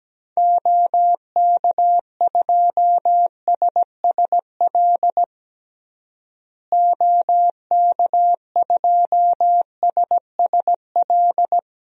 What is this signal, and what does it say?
Morse code, heard as a keyed tone, about 17 wpm, 700 Hz. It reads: OK2SSL OK2SSL
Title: DXCC-100 Call Signs - 6 Characters - Repeated 2x - 17wpm